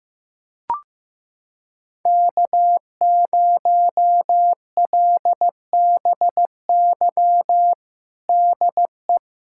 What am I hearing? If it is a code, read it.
Morse code, heard as a keyed tone, about 16 wpm, 700 Hz. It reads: K0LBY DE
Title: USA Callsign DE Callsign - 5 Characters - 15wpm